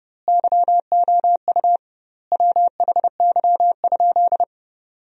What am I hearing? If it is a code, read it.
Morse code, heard as a keyed tone, about 32 wpm, 700 Hz. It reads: YOU WHY?
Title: 2x Word-Spacing - Sentences from Top 200 Words - 30wpm